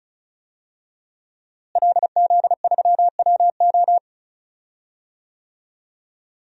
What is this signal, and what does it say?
Morse code, heard as a keyed tone, about 34 wpm, 700 Hz. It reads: LZ3WO
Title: DXCC-100 Call Signs - 5 Characters - Repeated 2x - 35wpm